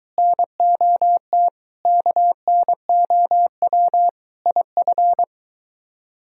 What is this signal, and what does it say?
Morse code, heard as a keyed tone, about 22 wpm, 700 Hz. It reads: NOT KNOW IF